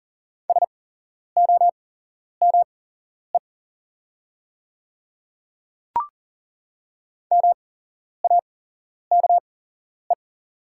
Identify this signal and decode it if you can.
Morse code, heard as a keyed tone, about 39 wpm, spaced as for 11 wpm, 700 Hz. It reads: SOME MAKE
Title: 8x Character Spacing - Top 100 Words - 40wpm